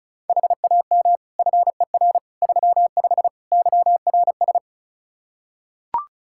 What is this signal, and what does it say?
Morse code, heard as a keyed tone, about 36 wpm, 700 Hz. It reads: HAM FER 35 YRS
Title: K1USN SST QSO Practice 35wpm